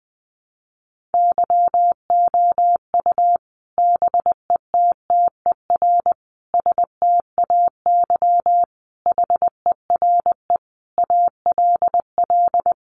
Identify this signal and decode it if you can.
Morse code, heard as a keyed tone, about 19 wpm, 700 Hz. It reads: YOU BETTER STAY HERE ALL
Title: Sentences from Top 500 Words 20wpm